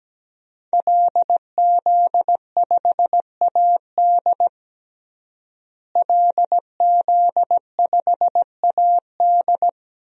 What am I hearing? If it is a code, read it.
Morse code, heard as a keyed tone, about 17 wpm, 700 Hz. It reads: LZ5AD LZ5AD
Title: DXCC-100 Call Signs - 5 Characters - Repeated 2x - 17wpm